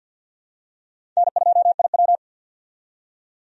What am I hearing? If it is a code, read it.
Morse code, heard as a keyed tone, about 52 wpm, 700 Hz. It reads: N2IW
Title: USA Call Signs - 4 Characters - 50wpm